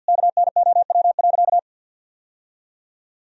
Morse code, heard as a keyed tone, about 50 wpm, 700 Hz. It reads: KNOW.
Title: Sentences from Top 1000 Words - Review Part 2 - 50wpm